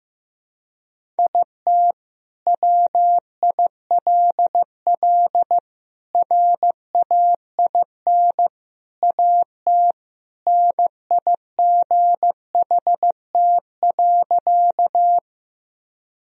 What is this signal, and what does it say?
Morse code, heard as a keyed tone, about 15 wpm, 700 Hz. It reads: IT WILL RAIN AT NIGHT.
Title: Sentences from Top 1000 Words - Review Part 1 - 15wpm